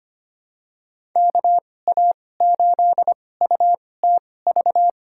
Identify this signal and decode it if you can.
Morse code, heard as a keyed tone, about 25 wpm, spaced as for 18 wpm, 700 Hz. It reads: KA8UTV
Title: USA Call Signs - 6 Characters - 2x Character Spacing - 25wpm